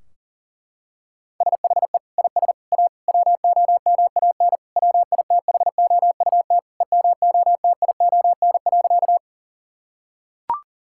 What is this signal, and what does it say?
Morse code, heard as a keyed tone, about 40 wpm, 700 Hz. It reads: SHE IS A WOMAN WITHOUT EMOTION.
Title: Sentences from Top 1600 Words 40wpm